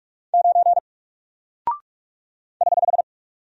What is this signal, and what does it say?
Morse code, heard as a keyed tone, about 46 wpm, 700 Hz. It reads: 9 <HH>
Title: Single Character (Letter:Number:Punctuation:Prosign) Alt 45wpm